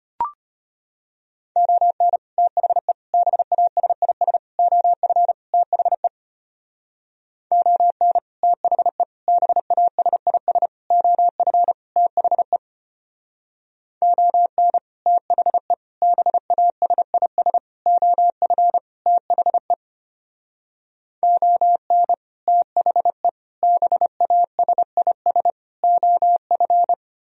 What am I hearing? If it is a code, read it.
Morse code, heard as a keyed tone, about 31 wpm, 700 Hz. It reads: ON THE BASIS OF THE ON THE BASIS OF THE ON THE BASIS OF THE ON THE BASIS OF
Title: Speed-Racing - Sets of 5 Words 25wpm